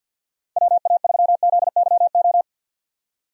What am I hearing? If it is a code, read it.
Morse code, heard as a keyed tone, about 50 wpm, 700 Hz. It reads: WA3ZYO
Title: USA Call Signs - 6 Characters - 50wpm